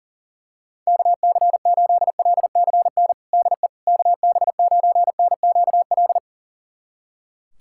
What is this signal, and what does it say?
Morse code, heard as a keyed tone, about 40 wpm, 700 Hz. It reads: KC8LCN DE KB9NQL
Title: USA Callsign DE Callsign - 6 Characters - 40wpm